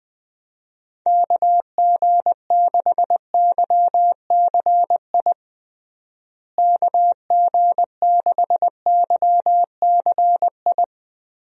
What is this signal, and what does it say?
Morse code, heard as a keyed tone, about 20 wpm, 700 Hz. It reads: KG6YCI KG6YCI
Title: USA Call Signs - 6 Characters - Repeated 3x - 20wpm